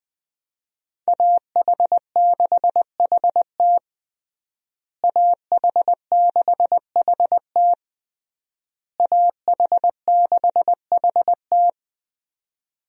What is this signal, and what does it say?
Morse code, heard as a keyed tone, about 20 wpm, 700 Hz. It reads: AH6HT AH6HT AH6HT
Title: USA Call Signs - 5 Characters - Repeated 3x - 20wpm